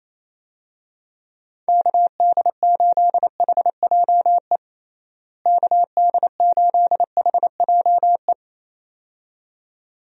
Morse code, heard as a keyed tone, about 28 wpm, 700 Hz. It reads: KD8HJE KD8HJE
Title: USA Call Signs - 6 Characters - Repeated 2x - 28wpm